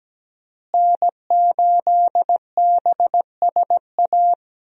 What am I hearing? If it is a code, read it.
Morse code, heard as a keyed tone, about 17 wpm, 700 Hz. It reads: N8BSA
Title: USA Call Signs - 5 Characters - Repeated 3x - 17wpm